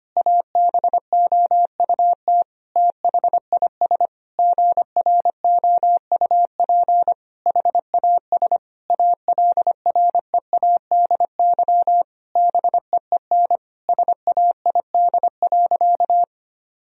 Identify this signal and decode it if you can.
Morse code, heard as a keyed tone, about 25 wpm, 700 Hz. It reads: ABOUT THIS GROUP HAS ALREADY BEEN SAID.
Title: Sentences from Top 1000 Words - Review Part 1 - 25wpm